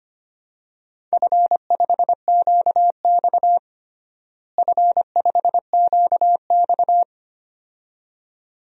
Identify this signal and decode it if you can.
Morse code, heard as a keyed tone, about 25 wpm, 700 Hz. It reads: F5QX F5QX
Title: DXCC-100 Call Signs - 4 Characters - Repeated 2x - 25wpm